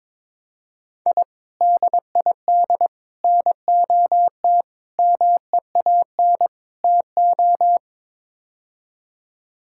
Morse code, heard as a keyed tone, about 22 wpm, 700 Hz. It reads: I DID NOT MEAN TO
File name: practice / 22wpm / Build A Phrase - 5-Word Phrases - 22wpm